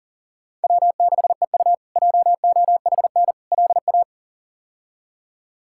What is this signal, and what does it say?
Morse code, heard as a keyed tone, about 37 wpm, 700 Hz. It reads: W6EU JOHN LA
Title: K1USN SST QSO Practice - 40wpm